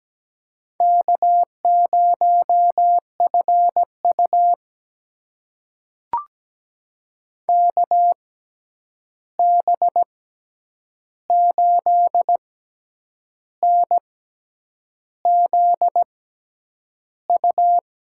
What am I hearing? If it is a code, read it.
Morse code, heard as a keyed tone, about 17 wpm, spaced as for 8 wpm, 700 Hz. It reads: K0FU KB8NZU
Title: USA Call Signs - 6x Character Spacing - 17wpm